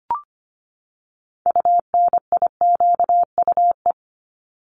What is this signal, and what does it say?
Morse code, heard as a keyed tone, about 25 wpm, 700 Hz. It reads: UNIQUE
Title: Top 751-1000 Words 25wpm